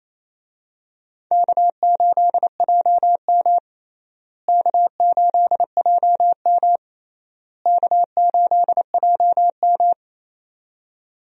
Morse code, heard as a keyed tone, about 28 wpm, 700 Hz. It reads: K8JM K8JM K8JM
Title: USA Call Signs - 4 Characters - Repeated 3x - 28wpm